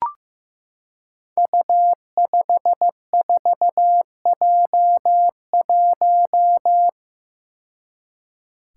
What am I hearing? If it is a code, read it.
Morse code, heard as a keyed tone, about 15 wpm, 700 Hz. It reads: U54J1
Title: Five Challenging Letter-Number Combinations 15wpm